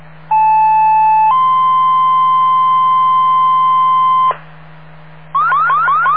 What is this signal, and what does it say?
Two-tone page, heard as a one-second tone, then a three-second tone, about 870 then 1040 Hz, with noise. It is heard from 0.3 to 4.3 s.